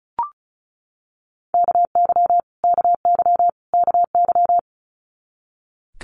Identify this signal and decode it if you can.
Morse code, heard as a keyed tone, about 34 wpm, 700 Hz. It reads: KY KY KY
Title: US States Repeated 3x 35wpm